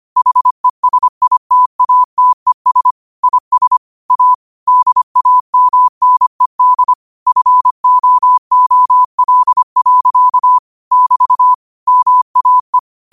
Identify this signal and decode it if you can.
Morse code, heard as a keyed tone, about 25 wpm, 1000 Hz. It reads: HESITATES IS A DAMNED FOOL. <BT> MAE
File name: qotdfiles / 25wpm